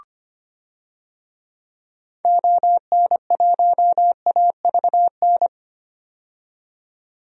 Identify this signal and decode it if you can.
Morse code, heard as a keyed tone, about 25 wpm, 700 Hz. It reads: ON1AVN
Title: DXCC-100 Call Signs - 6 Characters - 25wpm